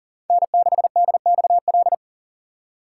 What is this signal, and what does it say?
Morse code, heard as a keyed tone, about 40 wpm, 700 Hz. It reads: N6DXL
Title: USA Call Signs - 5 Characters - 2x Character Spacing - 40wpm